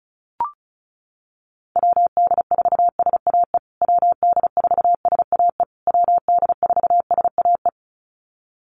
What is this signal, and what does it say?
Morse code, heard as a keyed tone, about 34 wpm, 700 Hz. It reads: WD4SAE WD4SAE WD4SAE
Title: USA Call Signs Repeated 3x 35wpm